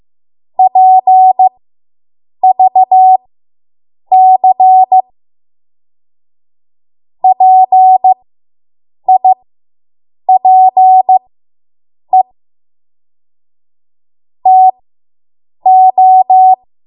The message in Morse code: PVC PIPE TO

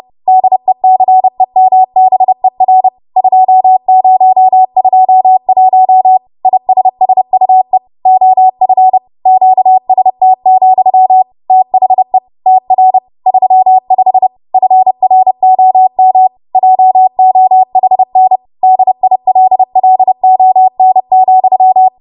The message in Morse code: DECEMBER 2021 ISSUE OF QST, THE TR 35 FROM JOHN DILLON,

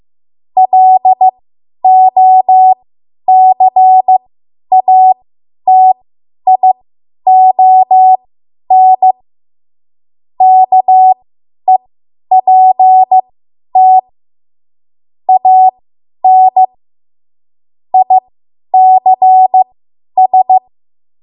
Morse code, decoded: LOCATION KEPT AN ICS

